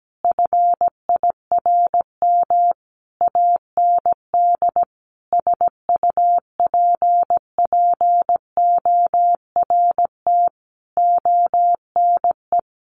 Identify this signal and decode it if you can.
Morse code, heard as a keyed tone, about 17 wpm, 700 Hz. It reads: FIRM AND SUPPORT ONE
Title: Sentences from Top 1400 Words 17wpm